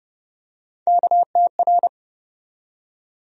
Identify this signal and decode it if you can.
Morse code, heard as a keyed tone, about 30 wpm, 700 Hz. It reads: KTR
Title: USA Call Sign Postfixes - 3 Letters 30wpm